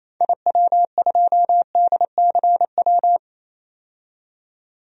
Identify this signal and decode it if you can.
Morse code, heard as a keyed tone, about 27 wpm, 700 Hz. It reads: IW2DCW
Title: DXCC-100 Call Signs - 6 Characters - 28wpm